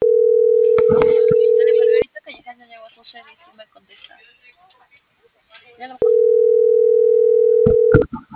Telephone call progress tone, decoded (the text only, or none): ringback tone